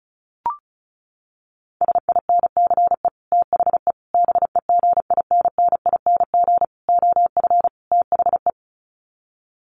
Morse code, heard as a keyed tone, about 34 wpm, 700 Hz. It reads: SINCE THE BEGINNING OF THE
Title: Sets of 5 Words Encore2 35wpm